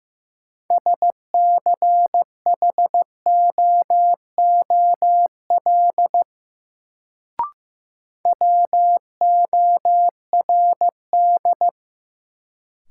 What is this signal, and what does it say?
Morse code, heard as a keyed tone, about 15 wpm, 700 Hz. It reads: SCHOOL WORD
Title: Sentences from Top 300 Words - New Word Drill - 15wpm